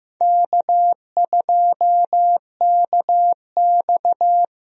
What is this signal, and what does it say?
Morse code, heard as a keyed tone, about 15 wpm, 700 Hz. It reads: K2KX
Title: USA Call Signs - 4 Characters - Repeated 3x - 15wpm